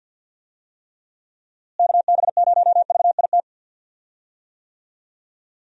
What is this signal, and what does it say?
Morse code, heard as a keyed tone, about 52 wpm, 700 Hz. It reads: KB0VIT
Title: USA Call Signs - 6 Characters - Repeated 3x - 50wpm